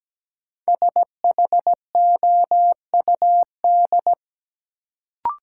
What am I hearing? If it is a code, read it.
Morse code, heard as a keyed tone, about 17 wpm, 700 Hz. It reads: SHOUD
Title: Sentences from Top 1000 Words - Review Part 4 - New Word Drill - 17wpm